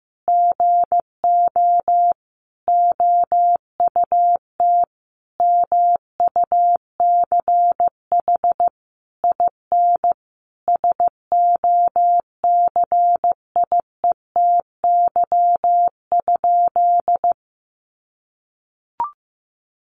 Morse code, heard as a keyed tone, about 15 wpm, 700 Hz. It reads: GO OUT MUCH IN SOCIETY?
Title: Sentences from Top 700 Words 15wpm